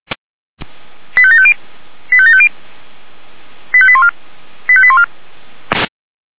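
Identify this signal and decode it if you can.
Five-tone selective call (ZVEI), heard as quick runs of five tones, tones of about 70 ms. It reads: 85750 85750 85814 85814
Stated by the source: ZVEI five-tone call received on SDR, from audio.f85360.feuerwehr-orlamuende.org fire brigade server.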